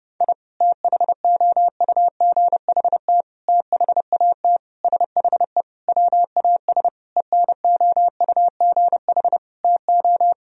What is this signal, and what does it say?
Morse code, heard as a keyed tone, about 32 wpm, 700 Hz. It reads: I THOUGHT THAT SHE WAS ENOUGH TO
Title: Sentences from Top 1000 Words - Review Part 3 - 30wpm